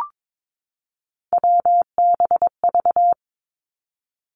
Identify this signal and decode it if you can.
Morse code, heard as a keyed tone, about 22 wpm, 700 Hz. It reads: WBV
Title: Three Challenging Letter-Number Combinations 22wpm